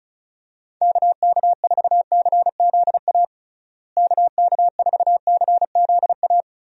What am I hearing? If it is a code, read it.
Morse code, heard as a keyed tone, about 34 wpm, 700 Hz. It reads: KK4CZA KK4CZA
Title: USA Call Signs - 6 Characters - Repeated 3x - 35wpm